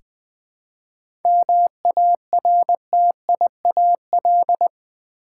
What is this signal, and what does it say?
Morse code, heard as a keyed tone, about 20 wpm, 700 Hz. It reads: MARTIAL
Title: 7 Letter Words - First 3 Letters Given - 20wpm